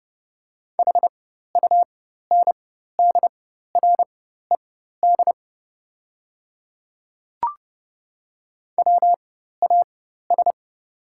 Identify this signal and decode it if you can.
Morse code, heard as a keyed tone, about 30 wpm, spaced as for 14 wpm, 700 Hz. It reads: HUNDRED WAS
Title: 4x Character Spacing - Top 500 Words - 30wpm